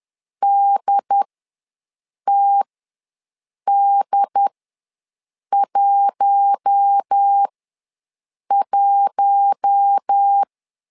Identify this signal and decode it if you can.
Morse code, heard as a keyed tone, about 10 wpm, spaced as for 6 wpm, 800 Hz. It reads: DTD11
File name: LEZIONI / 4 Q1 ws 10wpm 2538